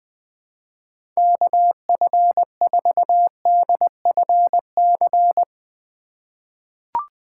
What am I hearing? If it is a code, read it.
Morse code, heard as a keyed tone, about 20 wpm, 700 Hz. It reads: KF4DFC
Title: USA Call Signs - 6 Characters - 6x Character Spacing - 20wpm